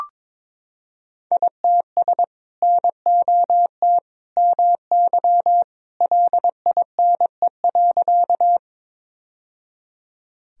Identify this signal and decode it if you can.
Morse code, heard as a keyed tone, about 22 wpm, 700 Hz. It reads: ITS NOT MY LINE.
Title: Sentences from Top 1000 Words - Review Part 1 - 22wpm